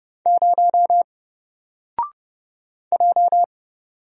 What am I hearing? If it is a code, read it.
Morse code, heard as a keyed tone, about 30 wpm, 700 Hz. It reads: 0 J
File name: Single-Letter-Number-30wpm